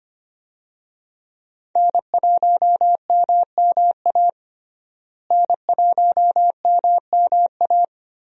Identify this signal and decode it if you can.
Morse code, heard as a keyed tone, about 25 wpm, 700 Hz. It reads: N1MMA N1MMA
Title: USA Call Signs - 5 Characters - Repeated 2x - 25wpm